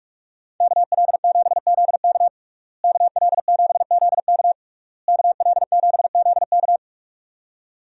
Morse code, heard as a keyed tone, about 41 wpm, 700 Hz. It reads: KL7ZK KL7ZK KL7ZK
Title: USA Call Signs - 5 Characters - Repeated 3x - 45wpm